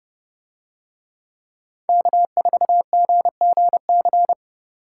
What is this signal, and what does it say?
Morse code, heard as a keyed tone, about 30 wpm, 700 Hz. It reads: K4GGC